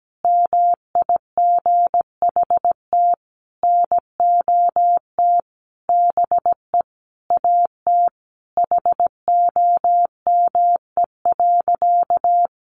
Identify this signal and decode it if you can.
Morse code, heard as a keyed tone, about 17 wpm, 700 Hz. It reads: MIGHT NOT BE AT HOME.
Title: Sentences from Top 1500 Words 17wpm